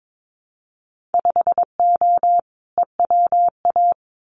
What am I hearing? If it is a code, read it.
Morse code, heard as a keyed tone, about 22 wpm, 700 Hz. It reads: 5O EWA
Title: Winter Field Day Exchanges Encore 22wpm